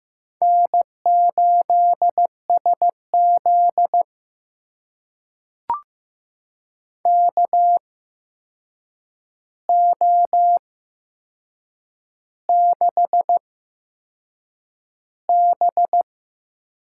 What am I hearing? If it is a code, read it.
Morse code, heard as a keyed tone, about 15 wpm, spaced as for 6 wpm, 700 Hz. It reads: N8SZ KO6B
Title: USA Call Signs - 4 Characters - 8x Character Spacing - 15wpm